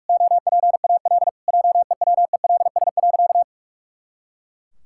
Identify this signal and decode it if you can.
Morse code, heard as a keyed tone, about 41 wpm, 700 Hz. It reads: OPAL JEWELS.
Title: Pangram Challenge v2 - 45wpm